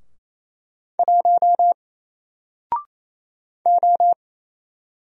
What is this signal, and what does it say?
Morse code, heard as a keyed tone, about 28 wpm, 700 Hz. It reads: 1 O
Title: Single Character (Letter:Number:Punctuation:Prosign) 28wpm